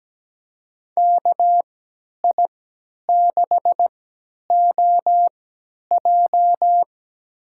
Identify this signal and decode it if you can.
Morse code, heard as a keyed tone, about 17 wpm, spaced as for 10 wpm, 700 Hz. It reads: KI6OJ
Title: USA Call Signs - 5 Characters - 3x Character Spacing - 17wpm